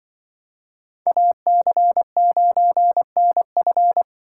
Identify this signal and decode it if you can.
Morse code, heard as a keyed tone, about 28 wpm, 700 Hz. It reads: AC9NF